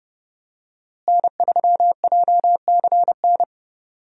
Morse code, heard as a keyed tone, about 31 wpm, 700 Hz. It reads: N3JCN
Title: USA Call Signs - 5 Characters - Repeated 2x - 30wpm